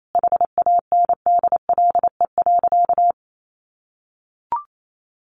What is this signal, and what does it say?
Morse code, heard as a keyed tone, about 28 wpm, 700 Hz. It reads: HANDLE.
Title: Sentences from Top 1100 Words 28wpm